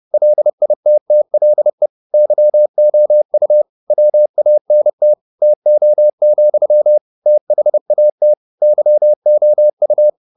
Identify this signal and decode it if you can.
Morse code, heard as a keyed tone, about 30 wpm, 600 Hz. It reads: LITTLE YOU WANT TO, THAT YOU
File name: the old man and the sea / the old man and the sea 1 30wpm